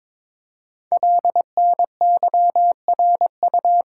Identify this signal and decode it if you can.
Morse code, heard as a keyed tone, about 22 wpm, 700 Hz. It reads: LNYRU
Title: Code Groups - 5 Letter:Numbers - 22wpm